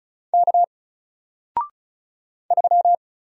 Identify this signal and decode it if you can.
Morse code, heard as a keyed tone, about 36 wpm, 700 Hz. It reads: K 3
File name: Single-Letter-Number-35wpm